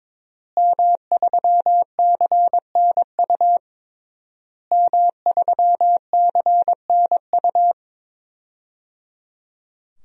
Morse code, heard as a keyed tone, about 21 wpm, 700 Hz. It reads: M3CNU M3CNU
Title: DXCC-100 Call Signs - 5 Characters - Repeated 3x - 22wpm